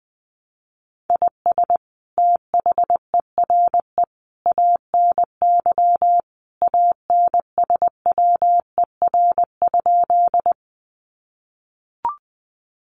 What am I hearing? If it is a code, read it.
Morse code, heard as a keyed tone, about 20 wpm, 700 Hz. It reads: IS THERE ANY ANSWER?
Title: Sentences from Top 300 Words 20wpm